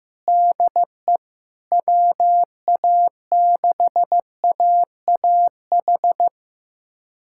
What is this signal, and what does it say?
Morse code, heard as a keyed tone, about 16 wpm, 700 Hz. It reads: DE WA6AAH
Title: USA Callsign DE Callsign 15wpm